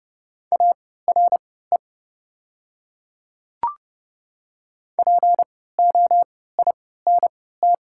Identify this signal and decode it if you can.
Morse code, heard as a keyed tone, about 29 wpm, spaced as for 17 wpm, 700 Hz. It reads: ARE POINT
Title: 3x Character Spacing - Top 200 Words - 30wpm